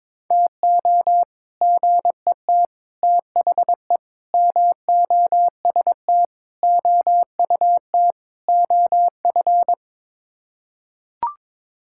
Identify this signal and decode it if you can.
Morse code, heard as a keyed tone, about 22 wpm, 700 Hz. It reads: TO GET THE MOST OUT OF
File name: practice / 22wpm / Sets of 6 Words - 22wpm